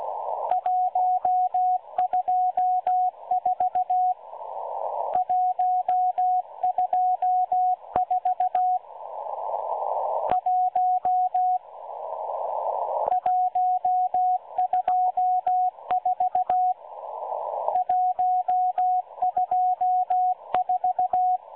Morse code, heard as a keyed tone, about 16 wpm, 730 Hz, with radio noise. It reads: 124 124 1 124 124
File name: M12-Morse-Numbers-Station-8116-kHz-CW